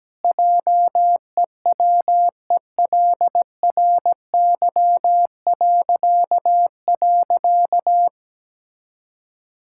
Morse code, heard as a keyed tone, about 17 wpm, 700 Hz. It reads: JEWELRY..
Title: Pangram Challenge v2 - 17wpm